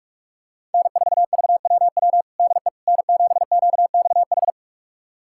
Morse code, heard as a keyed tone, about 41 wpm, 700 Hz. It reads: N4VWW DE N7QXH
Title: USA Callsign DE Callsign - 5 Characters - 45wpm